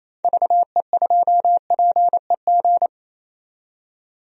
Morse code, heard as a keyed tone, about 28 wpm, 700 Hz. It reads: VE2PEG
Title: DXCC-100 Call Signs - 6 Characters - 28wpm